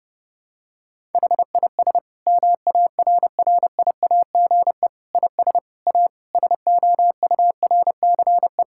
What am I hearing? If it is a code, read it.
Morse code, heard as a keyed tone, about 32 wpm, 700 Hz. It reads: HIS MARRIAGE IS A SOURCE
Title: Sentences from Top 1000 Words - Review Part 6 - 30wpm